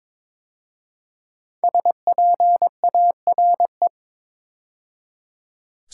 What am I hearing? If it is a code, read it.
Morse code, heard as a keyed tone, about 22 wpm, 700 Hz. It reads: SPARE
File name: 5 Letter Words - First 3 Letters Given - 22wpm